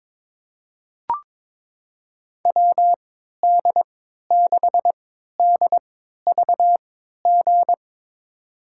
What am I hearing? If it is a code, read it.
Morse code, heard as a keyed tone, about 22 wpm, spaced as for 12 wpm, 700 Hz. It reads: WD6DVG
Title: USA Call Signs - 6 Characters - 3x Character Spacing - 22wpm